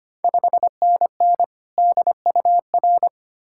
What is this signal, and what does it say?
Morse code, heard as a keyed tone, about 25 wpm, 700 Hz. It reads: 5NN DUR
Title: Ontario QSO Party Exchange 25wpm